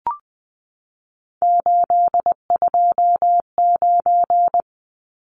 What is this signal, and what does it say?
Morse code, heard as a keyed tone, about 20 wpm, 700 Hz. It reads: 829
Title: Numbers - 3-Digit 20wpm